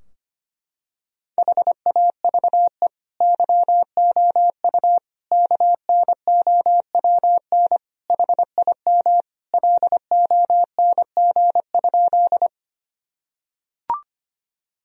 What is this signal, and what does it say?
Morse code, heard as a keyed tone, about 25 wpm, 700 Hz. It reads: HAVE YOU KNOWN HIM LONG?
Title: Sentences from Top 1000 Words - Review Part 3 - 25wpm